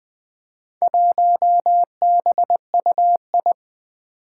Morse code, heard as a keyed tone, about 20 wpm, 700 Hz. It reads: 1BUI
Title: Four Challenging Letter-Number Combinations 20wpm